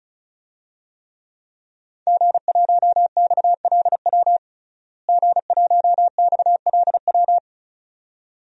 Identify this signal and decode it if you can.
Morse code, heard as a keyed tone, about 36 wpm, 700 Hz. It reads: G1XLW G1XLW
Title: DXCC-100 Call Signs - 5 Characters - Repeated 2x - 35wpm